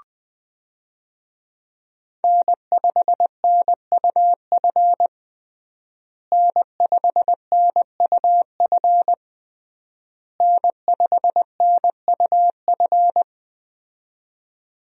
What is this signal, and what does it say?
Morse code, heard as a keyed tone, about 20 wpm, 700 Hz. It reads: N5NUF N5NUF N5NUF
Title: USA Call Signs - 5 Characters - Repeated 3x - 20wpm